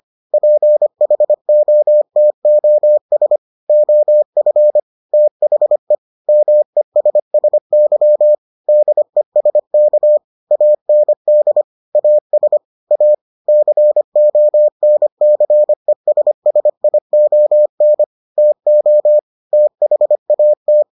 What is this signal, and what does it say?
Morse code, heard as a keyed tone, about 25 wpm, 600 Hz. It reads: PHOTOS OF THE MESSY DESK AND AS A CONCESSION TO THAT